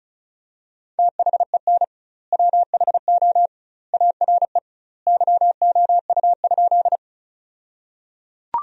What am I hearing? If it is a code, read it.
Morse code, heard as a keyed tone, about 36 wpm, 700 Hz. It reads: THEN WHO ARE YOU?
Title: 2x Word-Spacing - Sentences from Top 100 Words - 35wpm